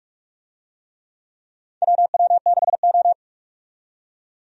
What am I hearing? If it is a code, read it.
Morse code, heard as a keyed tone, about 42 wpm, 700 Hz. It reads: WW6O
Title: USA Call Signs - 4 Characters - 45wpm